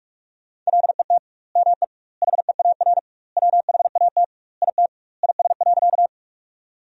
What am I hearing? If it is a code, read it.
Morse code, heard as a keyed tone, about 46 wpm, 700 Hz. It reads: LET ME HEAR WHAT IT IS.
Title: 2x Word-Spacing - Sentences from Top 300 Words - 45wpm